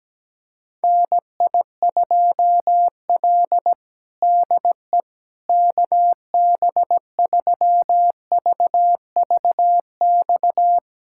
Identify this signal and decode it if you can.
Morse code, heard as a keyed tone, about 17 wpm, 700 Hz. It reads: NI2L DE KB3VVX
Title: USA Callsign DE Callsign 17wpm